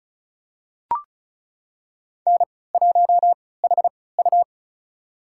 Morse code, heard as a keyed tone, about 36 wpm, spaced as for 20 wpm, 700 Hz. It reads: N1HU
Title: USA Call Signs - 4 Characters - 3x Character Spacing - 35wpm